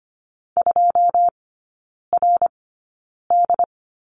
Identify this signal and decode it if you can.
Morse code, heard as a keyed tone, about 25 wpm, spaced as for 9 wpm, 690 Hz. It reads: 2RD
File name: Lesson 14e - Number 2 - Three Characters 25wpm